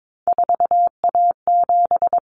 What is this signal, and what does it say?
Morse code, heard as a keyed tone, about 22 wpm, 700 Hz. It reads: 4A7
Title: Common Cut Numbers - 3-Digit 22wpm